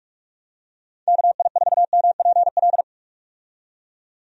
Morse code, heard as a keyed tone, about 42 wpm, 700 Hz. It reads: KI4MPL
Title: USA Call Signs - 6 Characters - 45wpm